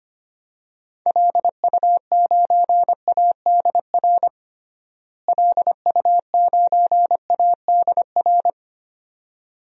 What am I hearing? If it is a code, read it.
Morse code, heard as a keyed tone, about 25 wpm, 700 Hz. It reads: LU9ADR LU9ADR
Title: DXCC-100 Call Signs - 6 Characters - Repeated 2x - 25wpm